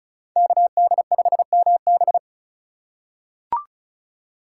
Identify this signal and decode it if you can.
Morse code, heard as a keyed tone, about 36 wpm, 700 Hz. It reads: KD5MB
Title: USA Call Signs - 5 Characters - 2x Character Spacing - 35wpm